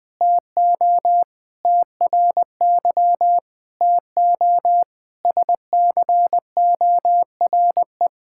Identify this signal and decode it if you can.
Morse code, heard as a keyed tone, about 20 wpm, 700 Hz. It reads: TO TRY TO SCORE
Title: Sentences from Top 1600 Words 20wpm